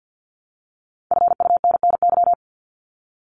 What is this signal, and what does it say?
Morse code, heard as a keyed tone, about 50 wpm, 700 Hz. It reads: FUNNY
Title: Top 1751-2000 Words 50wpm